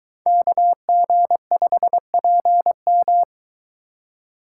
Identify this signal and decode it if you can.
Morse code, heard as a keyed tone, about 22 wpm, 700 Hz. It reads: KG5PM